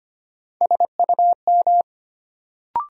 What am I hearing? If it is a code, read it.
Morse code, heard as a keyed tone, about 25 wpm, 700 Hz. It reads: SUM
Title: Sentences from Top 1000 Words - Review Part 6 - New Word Drill - 25wpm